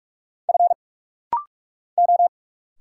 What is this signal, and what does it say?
Morse code, heard as a keyed tone, about 43 wpm, 700 Hz. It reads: F O
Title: Single Letter-Number Alt - Rapid-Fire 45wpm